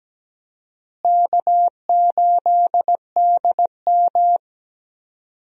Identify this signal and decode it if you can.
Morse code, heard as a keyed tone, about 17 wpm, 700 Hz. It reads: K8DM
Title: USA Call Signs - 4 Characters - Repeated 2x - 17wpm